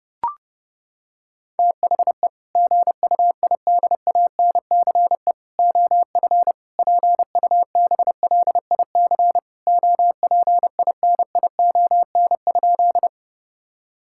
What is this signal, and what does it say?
Morse code, heard as a keyed tone, about 30 wpm, 700 Hz. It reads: THE GUIDANCE OF PUBLIC OPINION?
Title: Sentences from Top 1600 Words 30wpm